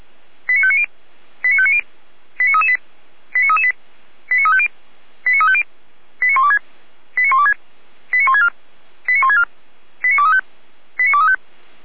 Five-tone selective call (ZVEI), heard as quick runs of five tones, tones of about 70 ms. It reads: 89590 89590 89338 89338 89350 89350 89126 89126 89164 89164 89235 89235